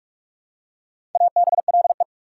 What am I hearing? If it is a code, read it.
Morse code, heard as a keyed tone, about 46 wpm, 690 Hz. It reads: ABLE
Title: Prefixes and Suffixes 45wpm